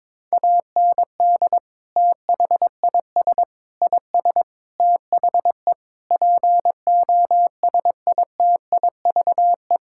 Morse code, heard as a keyed tone, about 22 wpm, 700 Hz. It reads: AND THIS IS THE POSITIVE
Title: Sentences from Top 1000 Words - Review Part 5 - 22wpm